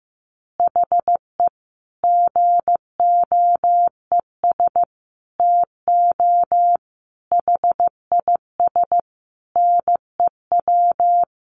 Morse code, heard as a keyed tone, about 15 wpm, 700 Hz. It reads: HE GOES TO HIS NEW
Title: Easy Phrases 15wpm